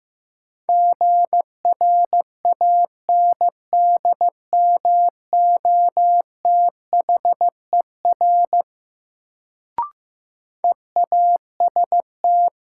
Morse code, heard as a keyed tone, about 15 wpm, 700 Hz. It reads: GRANDMOTHER EAST
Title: Sentences from Top 1200 Words - New Word Drill Alt - 15wpm